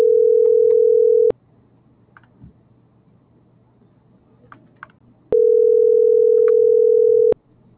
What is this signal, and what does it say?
Ringback tone (North America), 440 + 480 Hz.